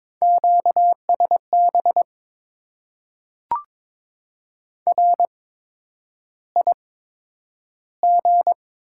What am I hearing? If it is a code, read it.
Morse code, heard as a keyed tone, about 22 wpm, spaced as for 9 wpm, 700 Hz. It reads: QSB RIG
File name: Top 90 QSO Elements - 8x Character Spacing - 22wpm